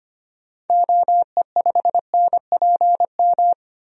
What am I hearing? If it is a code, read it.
Morse code, heard as a keyed tone, about 25 wpm, 700 Hz. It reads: OE5NPM
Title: DXCC-100 Call Signs - 6 Characters - 25wpm